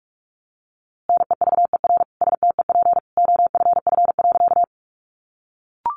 Morse code, heard as a keyed tone, about 44 wpm, 700 Hz. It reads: NEVER STEP OFF.
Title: A Sound Of Thunder 45wpm